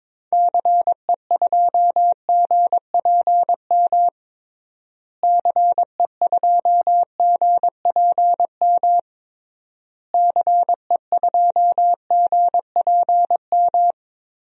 Morse code, heard as a keyed tone, about 22 wpm, 700 Hz. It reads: CE2GPM CE2GPM CE2GPM
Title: DXCC-100 Call Signs - 6 Characters - Repeated 3x - 22wpm